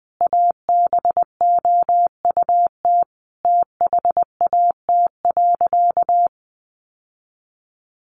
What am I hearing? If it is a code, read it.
Morse code, heard as a keyed tone, about 20 wpm, 700 Hz. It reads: ABOUT THAT.
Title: Sentences from Top 1000 Words 20wpm